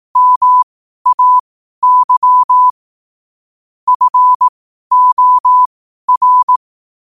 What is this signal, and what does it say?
Morse code, heard as a keyed tone, about 18 wpm, spaced as for 12 wpm, 1000 Hz. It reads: MAY FOR
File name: qotdfiles / 12wpm